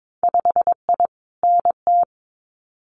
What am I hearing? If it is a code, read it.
Morse code, heard as a keyed tone, about 22 wpm, 700 Hz. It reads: 5I NT
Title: Winter Field Day Exchanges Encore 22wpm